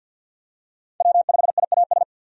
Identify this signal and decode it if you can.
Morse code, heard as a keyed tone, about 52 wpm, 700 Hz. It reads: W5IAS
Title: USA Call Signs - 5 Characters - Repeated 3x - 50wpm